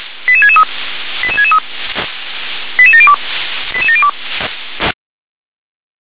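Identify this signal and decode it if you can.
Five-tone selective call (ZVEI), heard as quick runs of five tones, tones of about 80 ms. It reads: 88663 88663 88702 88702
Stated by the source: Five-tone ZVEI selective call received on SDR.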